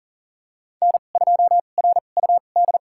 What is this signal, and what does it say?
Morse code, heard as a keyed tone, about 40 wpm, spaced as for 29 wpm, 700 Hz. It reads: N2RUD
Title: USA Call Signs - 2x Character Spacing - 40wpm